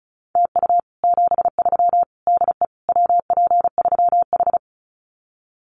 Morse code, heard as a keyed tone, about 34 wpm, 700 Hz. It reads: TU 73 DE WP3H
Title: Common QSO Phrases Encore 35wpm